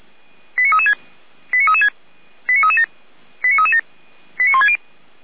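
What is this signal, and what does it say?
Five-tone selective call (ZVEI), heard as quick runs of five tones, tones of about 70 ms. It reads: 89336 89336 89337 89337 89160
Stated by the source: Five-tone ZVEI selective call 89165, received via SDR.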